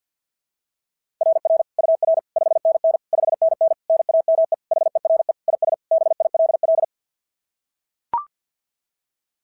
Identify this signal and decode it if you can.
Morse code, heard as a keyed tone, about 44 wpm, 660 Hz. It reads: RR UR 5NN 5NN NAME HERE IS BILL
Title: SOTA QSO Practice 50wpm